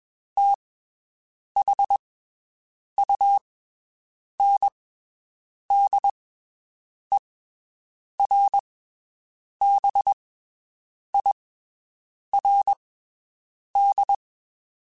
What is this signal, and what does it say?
Morse code, heard as a keyed tone, about 22 wpm, spaced as for 7 wpm, 800 Hz. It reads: THUNDERBIRD
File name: (spaced) / QSOs Spaced-mp3s / WD0BGP 1 @20 sp